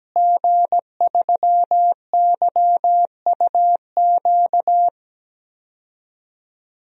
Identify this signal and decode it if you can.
Morse code, heard as a keyed tone, about 17 wpm, 700 Hz. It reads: G3YUQ
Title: DXCC-100 Call Signs - 5 Characters - Repeated 3x - 17wpm